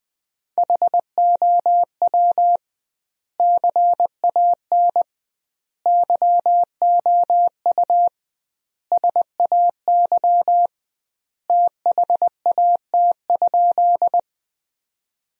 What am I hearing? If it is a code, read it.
Morse code, heard as a keyed tone, about 20 wpm, 700 Hz. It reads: HOW CAN YOU SAY THAT?
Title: 2x Word-Spacing - Sentences from Top 200 Words - 20wpm